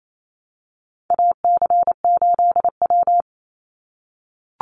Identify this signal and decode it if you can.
Morse code, heard as a keyed tone, about 28 wpm, 700 Hz. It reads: AC8W
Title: Call Signs - 28wpm